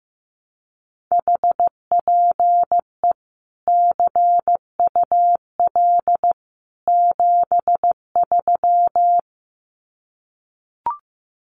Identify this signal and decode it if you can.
Morse code, heard as a keyed tone, about 15 wpm, 700 Hz. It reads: HPE CUL 73
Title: Common QSO Phrases 15wpm